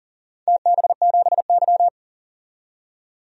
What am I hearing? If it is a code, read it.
Morse code, heard as a keyed tone, about 40 wpm, 700 Hz. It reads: TB7Y
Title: DXCC-100 Call Signs - 4 Characters - 40wpm